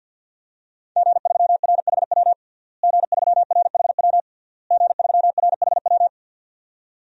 Morse code, heard as a keyed tone, about 46 wpm, 700 Hz. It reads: G3RHW G3RHW G3RHW
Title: DXCC-100 Call Signs - 5 Characters - Repeated 3x - 50wpm